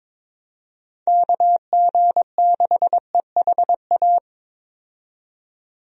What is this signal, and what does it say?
Morse code, heard as a keyed tone, about 21 wpm, 700 Hz. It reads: KG6EHA
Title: USA Call Signs - 6 Characters - 22wpm